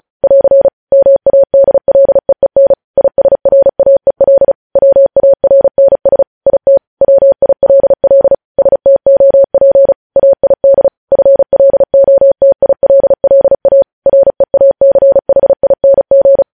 Morse code, heard as a keyed tone, about 36 wpm, 550 Hz. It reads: <AR> MADLEEN ISRAEL WARNS IT WILL STOP AID FLOTILLA REACHING
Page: news headlines in morse code at 35 wpm